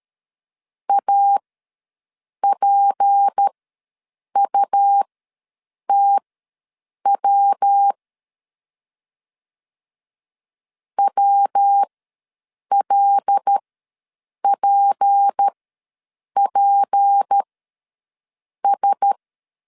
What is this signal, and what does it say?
Morse code, heard as a keyed tone, about 13 wpm, spaced as for 7 wpm, 800 Hz. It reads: APUTW WLPPS
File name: LEZIONI / 6 PKX0 / 6 P ws 12wpm 3117